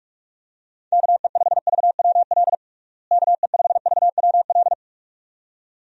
Morse code, heard as a keyed tone, about 42 wpm, 700 Hz. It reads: KE5VWL KE5VWL
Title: USA Call Signs - 6 Characters - Repeated 2x - 45wpm